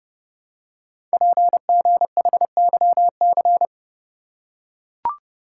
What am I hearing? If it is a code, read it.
Morse code, heard as a keyed tone, about 30 wpm, 700 Hz. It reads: PGHYC
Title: Code Groups - 5 Letters - 30wpm